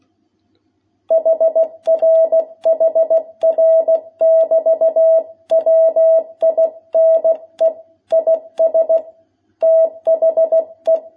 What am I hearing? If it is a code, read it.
Morse code, heard as a keyed tone, about 16 wpm, 660 Hz. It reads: HR HR <BT> WINE IS THE
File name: jh8cbh / 2021cw